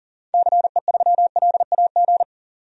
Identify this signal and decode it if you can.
Morse code, heard as a keyed tone, about 40 wpm, 700 Hz. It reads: CE3LAG
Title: DXCC-100 Call Signs - 6 Characters - 40wpm